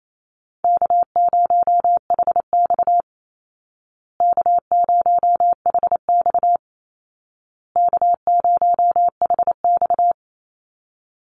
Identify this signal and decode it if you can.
Morse code, heard as a keyed tone, about 28 wpm, 700 Hz. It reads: K0HX K0HX K0HX
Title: Call Signs Repeated 2x - 28wpm